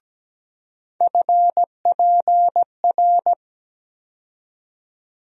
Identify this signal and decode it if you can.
Morse code, heard as a keyed tone, about 17 wpm, 700 Hz. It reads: FPR
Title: USA Call Sign Postfixes - 3 Letters 17wpm